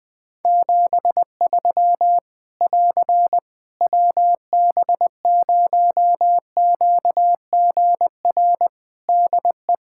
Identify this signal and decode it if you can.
Morse code, heard as a keyed tone, about 21 wpm, 700 Hz. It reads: 73 <AR> WB0QGR DE
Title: Practice QSO - Exchange 4 - Ending - 20wpm